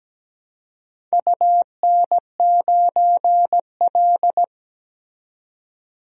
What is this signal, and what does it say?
Morse code, heard as a keyed tone, about 17 wpm, 700 Hz. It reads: UN9L
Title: DXCC-100 Call Signs - 4 Characters - 17wpm